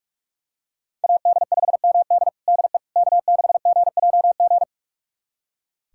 Morse code, heard as a keyed tone, about 46 wpm, 700 Hz. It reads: AD5MD DE K6GJG
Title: USA Callsign DE Callsign - 5 Characters - 45wpm